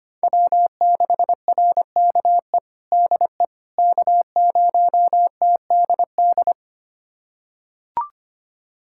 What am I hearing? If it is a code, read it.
Morse code, heard as a keyed tone, about 25 wpm, 700 Hz. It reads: W6RKE DE K0TDD
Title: USA Callsign DE Callsign - 5 Characters - 25wpm